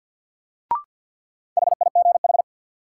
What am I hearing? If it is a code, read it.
Morse code, heard as a keyed tone, about 52 wpm, 700 Hz. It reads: HIGH
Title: Sentences from Top 200 Words - New Word Drill - 50wpm